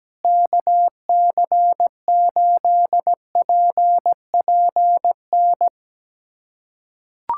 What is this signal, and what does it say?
Morse code, heard as a keyed tone, about 17 wpm, 700 Hz. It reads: KC8PPN
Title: USA Callsign DE Callsign - 6 Characters - 17wpm